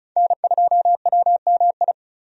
Morse code, heard as a keyed tone, about 36 wpm, 700 Hz. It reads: N2WMI
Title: USA Call Signs - 5 Characters - 35wpm